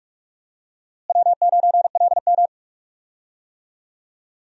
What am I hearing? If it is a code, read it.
Morse code, heard as a keyed tone, about 45 wpm, 700 Hz. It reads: W9LM
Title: USA Call Signs - 4 Characters - Repeated 2x - 45wpm